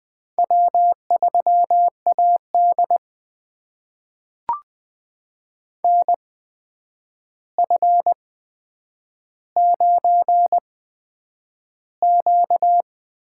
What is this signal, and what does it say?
Morse code, heard as a keyed tone, about 20 wpm, spaced as for 9 wpm, 700 Hz. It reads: W3AD NF9Q
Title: USA Call Signs - 8x Character Spacing - 20wpm